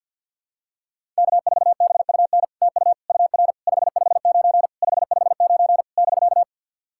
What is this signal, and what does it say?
Morse code, heard as a keyed tone, about 48 wpm, 700 Hz. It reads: K4BUN TU UR 559 559 <BK>
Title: SOTA QSO Practice 50wpm